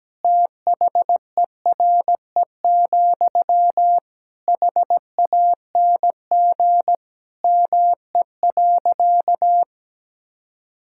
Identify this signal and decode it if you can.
Morse code, heard as a keyed tone, about 17 wpm, 700 Hz. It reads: THERE, HANG ME.
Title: Sentences from Top 1600 Words 17wpm